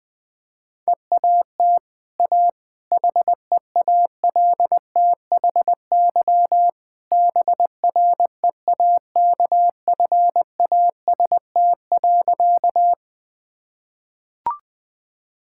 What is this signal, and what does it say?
Morse code, heard as a keyed tone, about 21 wpm, 700 Hz. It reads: EAT A HEALTHY BREAKFAST.
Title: Easy Sentences with Hints - 20wpm